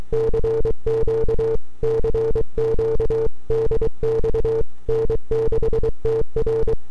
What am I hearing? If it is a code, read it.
Morse code, heard as a keyed tone, about 23 wpm, 450 Hz. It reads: CQ CQ DX N6TR
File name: n6tr / 12Oct2009-160